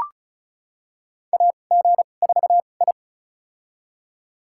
Morse code, heard as a keyed tone, about 34 wpm, spaced as for 25 wpm, 700 Hz. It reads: AG4I
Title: USA Call Signs - 4 Characters - 2x Character Spacing - 35wpm